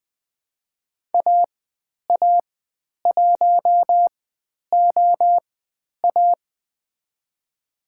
Morse code, heard as a keyed tone, about 20 wpm, spaced as for 10 wpm, 700 Hz. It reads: AA1OA
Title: 095 Lesson 16g - Letter c - Call Signs 20wpm